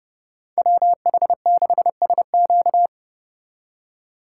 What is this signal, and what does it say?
Morse code, heard as a keyed tone, about 35 wpm, 700 Hz. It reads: WH6SQ
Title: Speed Racing - USA Call Signs 30wpm